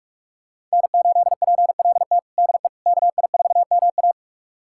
Morse code, heard as a keyed tone, about 41 wpm, 700 Hz. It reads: N8PLT DE KI4MA
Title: USA Callsign DE Callsign - 5 Characters - 45wpm